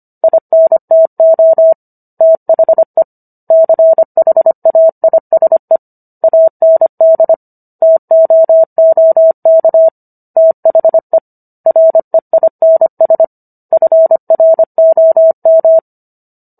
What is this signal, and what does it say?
Morse code, heard as a keyed tone, about 25 wpm, 650 Hz. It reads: INTO THE CHAISE AND TOOK THE REINS FROM